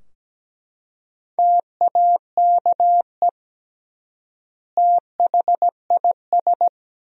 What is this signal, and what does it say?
Morse code, heard as a keyed tone, about 17 wpm, 700 Hz. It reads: TAKE THIS
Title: 3x Word-Spacing - Sentences from Top 200 Words - 17wpm